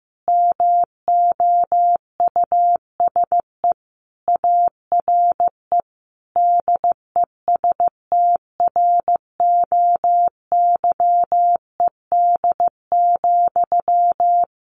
Morse code, heard as a keyed tone, about 15 wpm, 700 Hz. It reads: MOUSE ARE DESTROYED,
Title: A Sound Of Thunder 15wpm